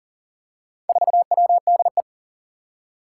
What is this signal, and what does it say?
Morse code, heard as a keyed tone, about 40 wpm, 700 Hz. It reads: 4WDE
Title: Four Challenging Letter-Number Combinations 40wpm